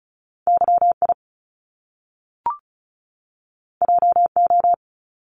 Morse code, heard as a keyed tone, about 36 wpm, 700 Hz. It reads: YI JO
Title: Top Two Letter Combinations in Words 35wpm